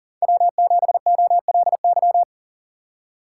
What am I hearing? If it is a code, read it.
Morse code, heard as a keyed tone, about 40 wpm, 690 Hz. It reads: W7OLY
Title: USA Call Signs - 4x Character Spacing - 40wpm